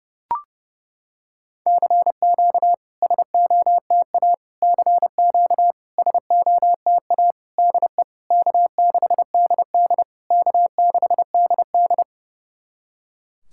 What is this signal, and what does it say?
Morse code, heard as a keyed tone, about 30 wpm, 690 Hz. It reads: CQ SOTA CQ SOTA DE K6DD K6DD
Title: SOTA QSO Practice 30wpm